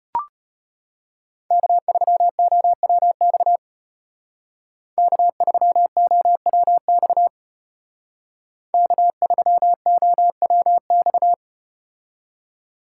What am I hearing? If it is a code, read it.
Morse code, heard as a keyed tone, about 32 wpm, 700 Hz. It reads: K3OWX K3OWX K3OWX